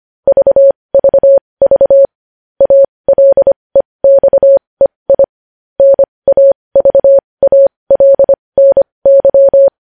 Morse code, heard as a keyed tone, about 25 wpm, spaced as for 20 wpm, 550 Hz. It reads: VVV ALEXEI NAVALNY
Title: News Headlines at 20